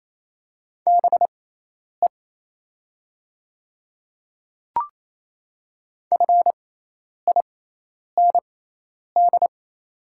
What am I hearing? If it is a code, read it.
Morse code, heard as a keyed tone, about 28 wpm, spaced as for 10 wpm, 690 Hz. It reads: BE FIND
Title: 6x Character Spacing - Top 200 Words - 28wpm